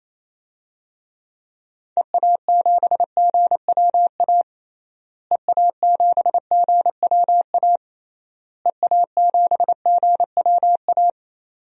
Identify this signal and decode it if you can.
Morse code, heard as a keyed tone, about 28 wpm, 700 Hz. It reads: EA7GWA EA7GWA EA7GWA
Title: DXCC-100 Call Signs - 6 Characters - Repeated 3x - 28wpm